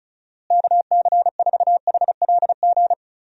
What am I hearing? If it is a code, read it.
Morse code, heard as a keyed tone, about 36 wpm, 700 Hz. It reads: KC4HLG
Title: USA Call Signs - 6 Characters - Repeated 3x - 35wpm